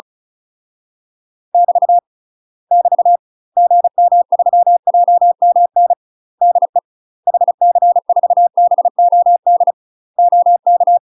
Morse code, heard as a keyed tone, about 34 wpm, 700 Hz. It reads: <BT> <BT> GM3JMN DE HC4BOD OK
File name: UK MASTER QSOs TRAINING / 35WPM / UK MASTER QSOs 0007